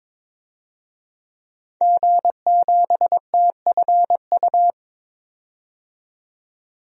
Morse code, heard as a keyed tone, about 22 wpm, 700 Hz. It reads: G7TFU
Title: DXCC-100 Call Signs - 5 Characters - 22wpm